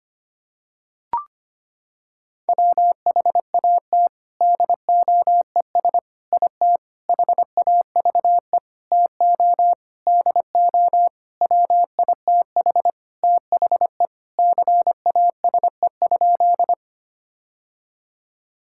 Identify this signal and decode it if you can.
Morse code, heard as a keyed tone, about 25 wpm, 700 Hz. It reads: WHAT DOES IT HAVE TO DO WITH THE CASE?
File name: practice / 25wpm / Sentences with 4-Word Phrases - 25wpm